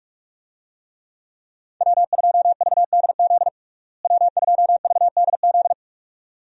Morse code, heard as a keyed tone, about 43 wpm, 700 Hz. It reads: W2VDZ W2VDZ
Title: USA Call Signs - 5 Characters - Repeated 2x - 45wpm